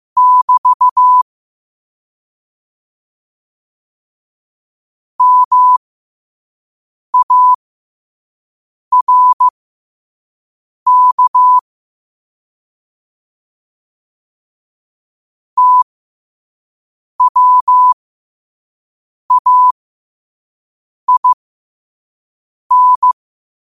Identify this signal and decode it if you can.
Morse code, heard as a keyed tone, about 15 wpm, spaced as for 5 wpm, 1000 Hz. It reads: <BT> MARK TWAIN
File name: 5wpm